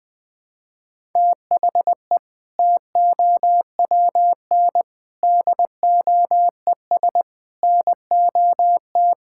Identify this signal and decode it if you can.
Morse code, heard as a keyed tone, about 20 wpm, 700 Hz. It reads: THE TOWN DOES NOT
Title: Sentences from Top 1000 Words - Review Part 5 - 20wpm